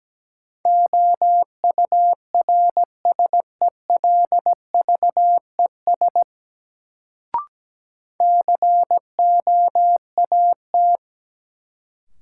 Morse code, heard as a keyed tone, about 17 wpm, 700 Hz. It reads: OURSELVES COAT
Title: Sentences from Top 1600 Words - New Word Drill - 17wpm